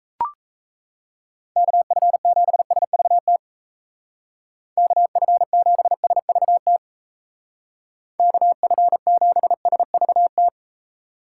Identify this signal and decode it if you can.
Morse code, heard as a keyed tone, about 35 wpm, 700 Hz. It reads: KF7SVT KF7SVT KF7SVT
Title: Speed Racing - USA 2x3 Call Signs 28wpm